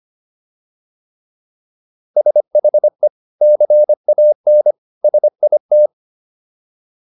Morse code, heard as a keyed tone, about 25 wpm, 600 Hz. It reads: SHE CAN SIT